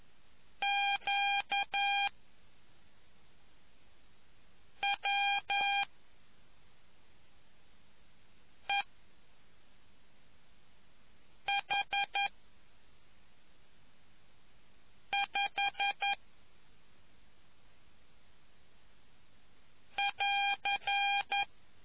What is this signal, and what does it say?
Morse code, heard as a keyed tone, about 11 wpm, spaced as for 3 wpm, 800 Hz. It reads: QWEH5<AR>